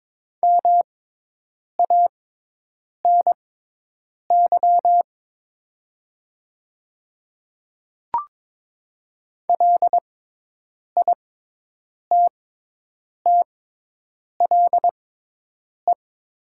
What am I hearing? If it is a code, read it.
Morse code, heard as a keyed tone, about 22 wpm, spaced as for 8 wpm, 690 Hz. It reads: MANY LITTLE
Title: 6x Character Spacing - Top 300 Words - 22wpm